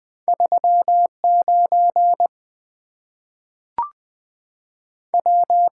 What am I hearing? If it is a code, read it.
Morse code, heard as a keyed tone, about 20 wpm, 700 Hz. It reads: 3 9 W
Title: Two Letter-Number Combinations - 20wpm